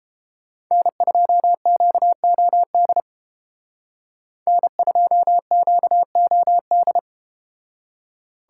Morse code, heard as a keyed tone, about 32 wpm, 700 Hz. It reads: N2QOD N2QOD